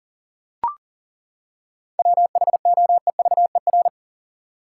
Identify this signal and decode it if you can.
Morse code, heard as a keyed tone, about 40 wpm, 700 Hz. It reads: WHOEVER
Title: Seven Letter Words - 40wpm